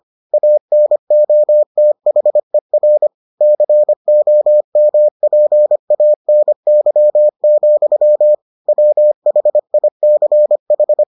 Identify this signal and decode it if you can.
Morse code, heard as a keyed tone, about 25 wpm, 600 Hz. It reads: ANOTHER COMPANY, WHICH